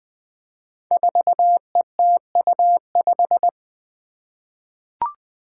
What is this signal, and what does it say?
Morse code, heard as a keyed tone, about 20 wpm, 700 Hz. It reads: 4ETU5
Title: Five Challenging Letter-Number Combinations 20wpm